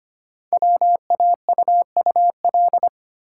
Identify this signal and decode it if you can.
Morse code, heard as a keyed tone, about 25 wpm, 700 Hz. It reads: WAUUL
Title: Code Groups - 5 Letters - 25wpm